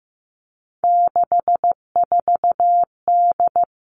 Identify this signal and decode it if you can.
Morse code, heard as a keyed tone, about 15 wpm, 700 Hz. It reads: 64D
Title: Three Challenging Letter-Number Combinations 15wpm